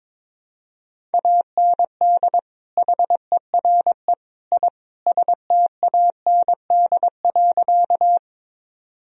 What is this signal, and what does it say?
Morse code, heard as a keyed tone, about 22 wpm, 700 Hz. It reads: AND HERE I STAND.
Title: Sentences from Top 1000 Words - Review Part 1 - 22wpm